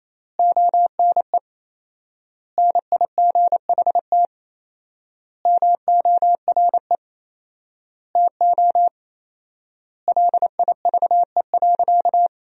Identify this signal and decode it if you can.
Morse code, heard as a keyed tone, about 28 wpm, 700 Hz. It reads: ONE NIGHT MORE TO LIVE.
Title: 4x Word-Spacing - Sentences from Top 300 Words - 28wpm